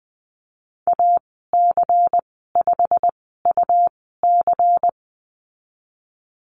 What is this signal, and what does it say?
Morse code, heard as a keyed tone, about 20 wpm, spaced as for 14 wpm, 700 Hz. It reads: AC5UC
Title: 122 Lesson 20g - Number 3 - Call Signs 20wpm